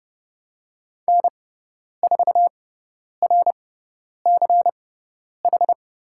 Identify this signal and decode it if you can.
Morse code, heard as a keyed tone, about 30 wpm, spaced as for 10 wpm, 700 Hz. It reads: N4RCH
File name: course / 30:10wpm / 102 Lesson 17g - Letter u - Call Signs 30wpm